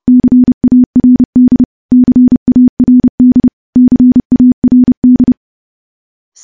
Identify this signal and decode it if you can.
Morse code, heard as a keyed tone, about 30 wpm, 250 Hz. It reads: CARD CARD CARD
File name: morse / morse-training-30WPM